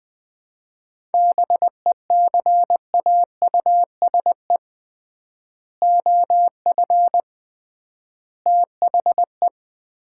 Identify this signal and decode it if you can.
Morse code, heard as a keyed tone, about 21 wpm, 700 Hz. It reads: BECAUSE OF THE
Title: 3x Word-Spacing - Sets of 3 Words - 20wpm